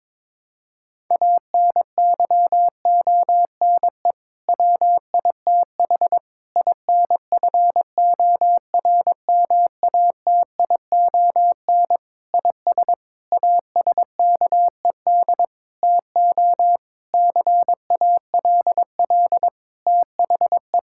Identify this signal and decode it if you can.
Morse code, heard as a keyed tone, about 22 wpm, 700 Hz. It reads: ANYONE WITH INFORMATION IS ASKED TO CALL THE
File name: Sentences with 7-Word Phrases - 22wpm